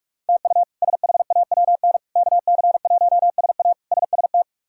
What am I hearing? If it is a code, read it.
Morse code, heard as a keyed tone, about 46 wpm, 700 Hz. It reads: TU SHAWN KC1SA SST
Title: K1USN SST QSO Practice - 45wpm